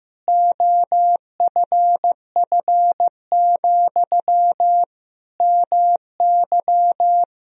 Morse code, heard as a keyed tone, about 15 wpm, 700 Hz. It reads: OFF, MY
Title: Sentences from Top 1000 Words - Review Part 2 - 15wpm